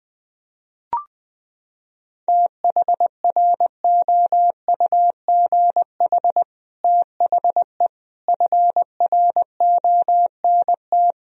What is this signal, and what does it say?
Morse code, heard as a keyed tone, about 21 wpm, 700 Hz. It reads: THROUGH THE FRONT
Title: Sentences from Top 1000 Words - Review Part 2 - 20wpm